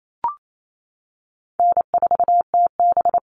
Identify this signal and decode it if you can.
Morse code, heard as a keyed tone, about 28 wpm, 700 Hz. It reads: N4TB
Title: Call Signs - 28wpm